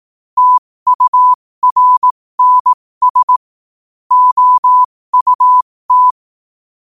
Morse code, heard as a keyed tone, about 18 wpm, spaced as for 15 wpm, 1000 Hz. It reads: TURNS OUT
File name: qotdfiles / 15wpm